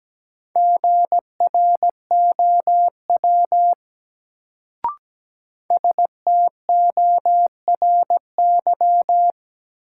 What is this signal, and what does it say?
Morse code, heard as a keyed tone, about 17 wpm, 700 Hz. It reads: GROW STORY
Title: Sentences from Top 300 Words - New Word Drill - 17wpm